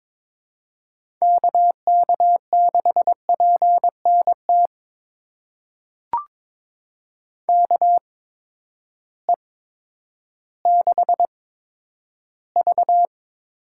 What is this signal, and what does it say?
Morse code, heard as a keyed tone, about 22 wpm, spaced as for 11 wpm, 700 Hz. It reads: KK6PNT KE6V
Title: USA Call Signs - 6 Characters - 8x Character Spacing - 22wpm